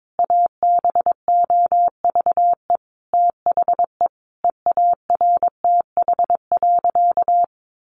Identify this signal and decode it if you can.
Morse code, heard as a keyed tone, about 22 wpm, 700 Hz. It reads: ABOVE THE EARTH.
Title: A Sound Of Thunder 22wpm